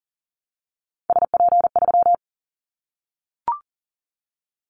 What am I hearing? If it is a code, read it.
Morse code, heard as a keyed tone, about 37 wpm, 700 Hz. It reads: SP3
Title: Callsign Prefixes for Top 25 DXCC - Repeated 3x - 40wpm